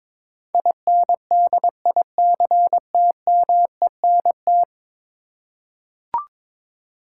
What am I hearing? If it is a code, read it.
Morse code, heard as a keyed tone, about 22 wpm, 700 Hz. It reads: INDICTMENT
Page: morse-code-ninja-practice-22wpm